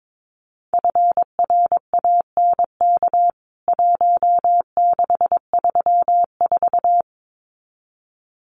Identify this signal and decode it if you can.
Morse code, heard as a keyed tone, about 22 wpm, 700 Hz. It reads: FRANK 1634
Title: CWT Exchange 22wpm